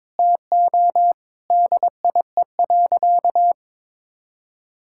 Morse code, heard as a keyed tone, about 22 wpm, 700 Hz. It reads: TO DIE.
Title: Sentences from Top 1600 Words 22wpm